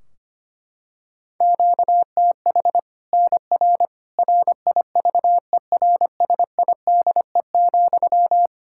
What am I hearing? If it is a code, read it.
Morse code, heard as a keyed tone, about 25 wpm, 700 Hz. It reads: QTH NR RIVERSIDE,
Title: Practice QSO - Exchange 1 - Beginning - 25wpm